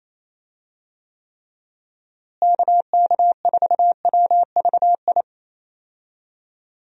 Morse code, heard as a keyed tone, about 28 wpm, 700 Hz. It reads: KK4WVI
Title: USA Call Signs - 6 Characters - Repeated 3x - 28wpm